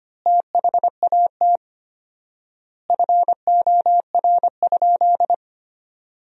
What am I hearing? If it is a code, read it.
Morse code, heard as a keyed tone, about 25 wpm, 700 Hz. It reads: THAT FOR?
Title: 4x Word-Spacing - Sentences from Top 200 Words - 25wpm